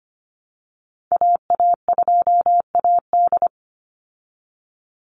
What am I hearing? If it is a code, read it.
Morse code, heard as a keyed tone, about 25 wpm, 700 Hz. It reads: AA2AD
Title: Call Signs Repeated 2x - 25wpm